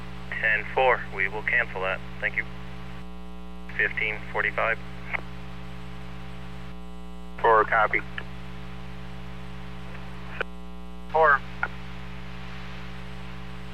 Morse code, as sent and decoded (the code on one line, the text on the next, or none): none